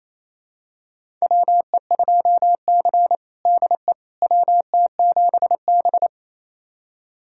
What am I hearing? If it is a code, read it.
Morse code, heard as a keyed tone, about 28 wpm, 700 Hz. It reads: WE2C DE WT7B
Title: USA Callsign DE Callsign - 4 Characters - 28wpm